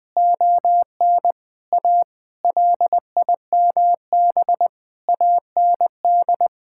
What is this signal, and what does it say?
Morse code, heard as a keyed tone, about 19 wpm, 700 Hz. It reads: ON A LIMB AND
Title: Sentences with 8-Word Phrases - Spoken Beginning - 20wpm